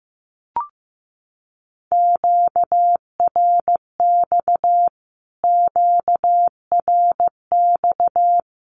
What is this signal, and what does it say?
Morse code, heard as a keyed tone, about 15 wpm, 700 Hz. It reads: QRX QRX
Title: Q-Codes Repeated 3x 15wpm